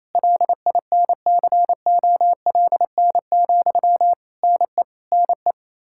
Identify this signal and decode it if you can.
Morse code, heard as a keyed tone, about 28 wpm, 700 Hz. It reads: LINCOLN, NE NE
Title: Practice QSO - Exchange 1 - Beginning - 28wpm